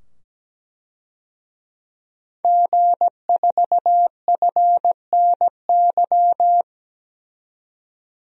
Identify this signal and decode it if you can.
Morse code, heard as a keyed tone, about 17 wpm, 700 Hz. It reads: G4FNY
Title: DXCC-100 Call Signs - 5 Characters - 17wpm